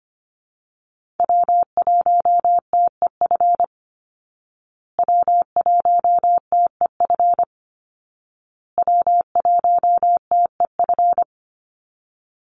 Morse code, heard as a keyed tone, about 25 wpm, 700 Hz. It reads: W1TEF W1TEF W1TEF
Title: Call Signs Repeated 2x - 25wpm